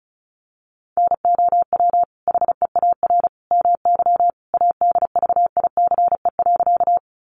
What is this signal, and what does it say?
Morse code, heard as a keyed tone, about 34 wpm, 700 Hz. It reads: NOW HEAR MY ADVICE.
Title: Sentences from Top 1100 Words 35wpm